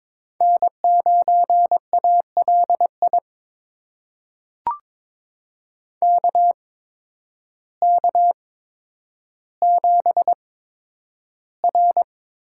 Morse code, heard as a keyed tone, about 22 wpm, spaced as for 10 wpm, 700 Hz. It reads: N9ALI KK7R
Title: USA Call Signs - 8x Character Spacing - 22wpm